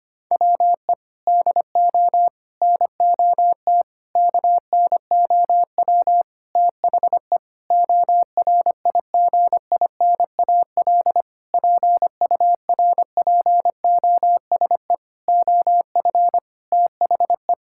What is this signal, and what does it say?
Morse code, heard as a keyed tone, about 25 wpm, 700 Hz. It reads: WE DO NOT KNOW THE ORIGINAL PURPOSE OF THE
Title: Sentences from Top 1000 Words - Review Part 6 - 25wpm